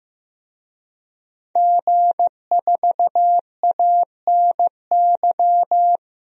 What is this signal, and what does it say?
Morse code, heard as a keyed tone, about 16 wpm, 700 Hz. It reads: G4ANY
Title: DXCC-100 Call Signs - 5 Characters - 15wpm